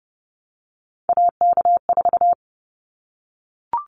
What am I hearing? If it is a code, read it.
Morse code, heard as a keyed tone, about 30 wpm, 700 Hz. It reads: AK4
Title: Callsign Prefixes for Top 25 DXCC - 30wpm